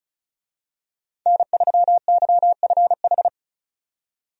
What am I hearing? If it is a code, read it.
Morse code, heard as a keyed tone, about 36 wpm, 700 Hz. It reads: N3YFH
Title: USA Call Signs - 5 Characters - 35wpm